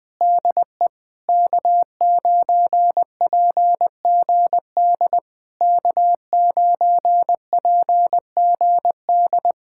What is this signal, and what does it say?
Morse code, heard as a keyed tone, about 20 wpm, 700 Hz. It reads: DE K9PGD K9PGD
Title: Speed-Racing Common QSO Phrases - 17wpm